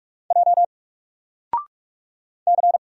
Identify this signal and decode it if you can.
Morse code, heard as a keyed tone, about 48 wpm, 700 Hz. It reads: J C
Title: Single Letter-Number Alt 45wpm